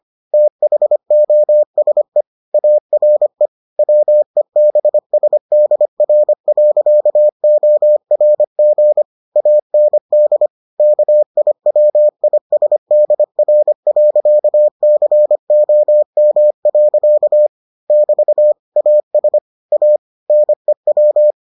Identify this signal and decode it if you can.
Morse code, heard as a keyed tone, about 25 wpm, 600 Hz. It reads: THOSE ARE WEBSDR.ORG AND KIWISDR.COM. <BT> AS A NEW